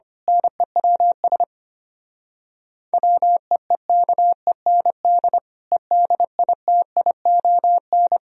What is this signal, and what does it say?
Morse code, heard as a keyed tone, about 25 wpm, 700 Hz. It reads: NEWS WEEKEND EDITION
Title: Speed-Racing - Sets of 5 Words 25wpm